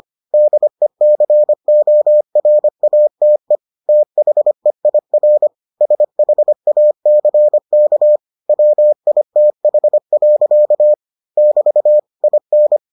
Morse code, heard as a keyed tone, about 25 wpm, 600 Hz. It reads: DECORATE THEIR SHACK WITH. <BT> IN